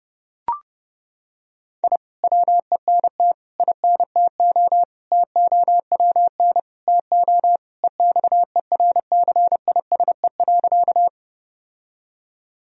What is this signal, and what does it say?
Morse code, heard as a keyed tone, about 32 wpm, 700 Hz. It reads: I WENT INTO TOWN TO EXERCISE.
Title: Sentences from Top 1000 Words - Review Part 4 - 30wpm